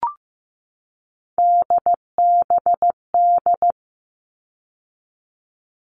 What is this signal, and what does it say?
Morse code, heard as a keyed tone, about 15 wpm, 700 Hz. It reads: DBD
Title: All Cut Numbers - 3-Digit 15wpm